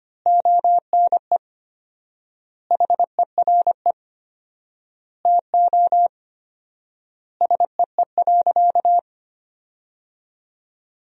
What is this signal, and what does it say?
Morse code, heard as a keyed tone, about 25 wpm, 700 Hz. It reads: ONE HERE TO SEE.
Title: 4x Word-Spacing - Sentences from Top 200 Words - 25wpm